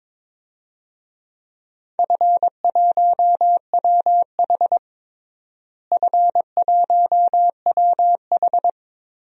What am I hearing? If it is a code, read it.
Morse code, heard as a keyed tone, about 22 wpm, 700 Hz. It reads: F1WH F1WH
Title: DXCC-100 Call Signs - 4 Characters - Repeated 3x - 22wpm